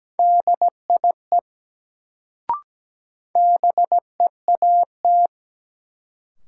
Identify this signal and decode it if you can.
Morse code, heard as a keyed tone, about 17 wpm, 700 Hz. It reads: DIE BEAT
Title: Sentences from Top 1300 Words - New Word Drill - 17wpm